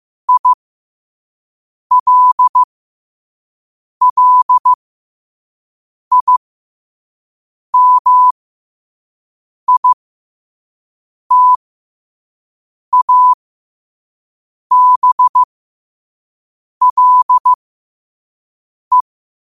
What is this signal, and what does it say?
Morse code, heard as a keyed tone, about 15 wpm, spaced as for 5 wpm, 1000 Hz. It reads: ILLIMITABLE
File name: qotdfiles / 5wpm